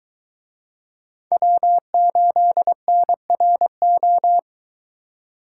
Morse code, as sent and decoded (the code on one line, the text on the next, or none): .-- ---.. -. .-. ---
W8NRO